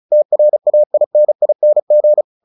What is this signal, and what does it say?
Morse code, heard as a keyed tone, about 34 wpm, 600 Hz. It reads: TRAINING
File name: the old man and the sea 2 35wpm